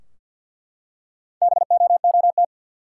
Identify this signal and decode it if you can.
Morse code, heard as a keyed tone, about 46 wpm, 700 Hz. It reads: BOOT
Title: Four Letter Words - 50wpm